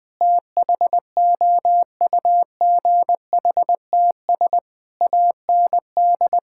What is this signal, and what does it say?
Morse code, heard as a keyed tone, about 20 wpm, 700 Hz. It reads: THOUGHTS AND
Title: Sets of 8 Words - 20wpm